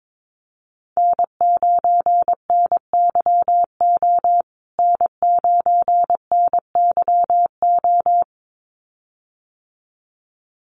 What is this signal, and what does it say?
Morse code, heard as a keyed tone, about 22 wpm, 700 Hz. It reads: N9NYO N9NYO
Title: USA Call Signs Repeated 2x 22wpm